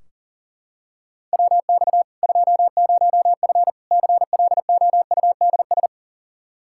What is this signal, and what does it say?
Morse code, heard as a keyed tone, about 40 wpm, 700 Hz. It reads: WX 20F CLOUDS
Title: Practice QSO - Exchange 2 - Info - 40wpm